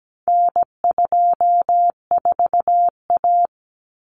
Morse code, heard as a keyed tone, about 17 wpm, 700 Hz. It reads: N24A
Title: Common Cut Numbers - 4-Digit 17wpm